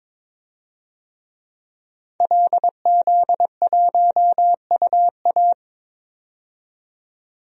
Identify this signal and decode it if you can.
Morse code, heard as a keyed tone, about 22 wpm, 700 Hz. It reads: LZ1UA
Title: DXCC-100 Call Signs - 5 Characters - 22wpm